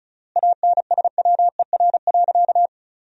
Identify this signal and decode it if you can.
Morse code, heard as a keyed tone, about 36 wpm, 700 Hz. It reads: ANSWER.
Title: 3x Word-Spacing - Sentences from Top 500 Words - 35wpm